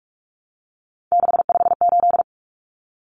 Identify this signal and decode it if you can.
Morse code, heard as a keyed tone, about 44 wpm, 700 Hz. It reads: 658
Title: Numbers - 3-Digit 45wpm